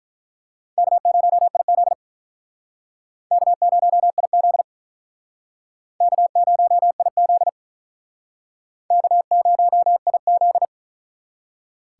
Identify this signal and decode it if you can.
Morse code, heard as a keyed tone, about 48 wpm, 700 Hz. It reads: K0IZ K0IZ K0IZ K0IZ